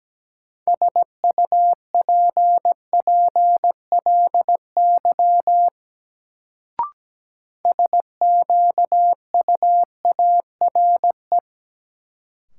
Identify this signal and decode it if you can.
Morse code, heard as a keyed tone, about 17 wpm, 700 Hz. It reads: SUPPLY SQUARE
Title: Sentences from Top 1200 Words - New Word Drill Alt - 17wpm